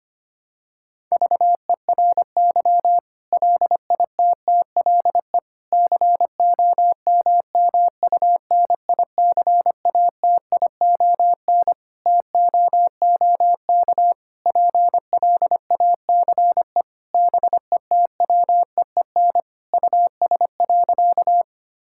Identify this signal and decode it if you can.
Morse code, heard as a keyed tone, about 25 wpm, 700 Hz. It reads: VERY LITTLE COMMUNICATION TOOK PLACE BETWEEN US.
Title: Sentences from Top 1000 Words - Review Part 6 - 25wpm